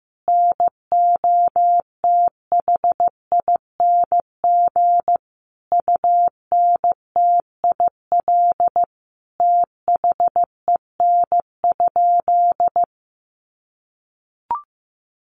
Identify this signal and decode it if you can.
Morse code, heard as a keyed tone, about 15 wpm, 700 Hz. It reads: NOTHING UNTIL THEN?
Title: Sentences from Top 500 Words 15wpm